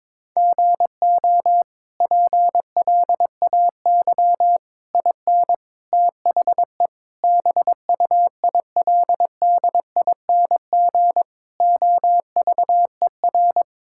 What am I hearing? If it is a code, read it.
Morse code, heard as a keyed tone, about 22 wpm, 700 Hz. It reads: GO PLAY IN THE BUILDING OVER
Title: Sentences from Top 1000 Words - Review Part 5 - 22wpm